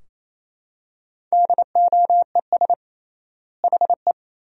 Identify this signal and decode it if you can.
Morse code, heard as a keyed tone, about 28 wpm, 700 Hz. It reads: DOES HE
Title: 3x Word-Spacing - Sentences from Top 400 Words - 28wpm